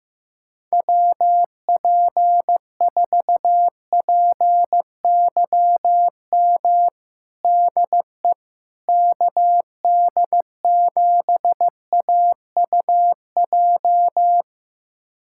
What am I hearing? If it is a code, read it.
Morse code, heard as a keyed tone, about 16 wpm, 700 Hz. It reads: WP4PYM DE KD7AUJ
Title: USA Callsign DE Callsign - 6 Characters - 15wpm